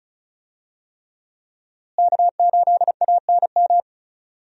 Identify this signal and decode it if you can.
Morse code, heard as a keyed tone, about 34 wpm, 700 Hz. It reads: K8ANM
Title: USA Call Signs - 5 Characters - Repeated 3x - 35wpm